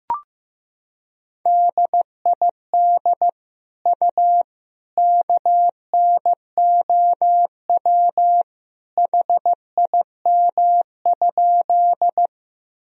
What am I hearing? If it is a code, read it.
Morse code, heard as a keyed tone, about 15 wpm, 700 Hz. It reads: DID U KNOW HIM?
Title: Sentences from Top 100 Words and Abbreviations 15wpm